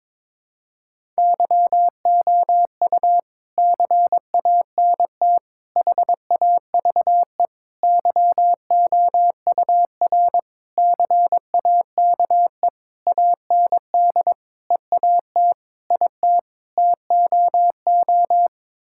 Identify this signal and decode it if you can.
Morse code, heard as a keyed tone, about 23 wpm, 700 Hz. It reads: YOU CANT HAVE YOUR CAKE AND EAT IT TOO
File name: Sets of 9 Words - 22wpm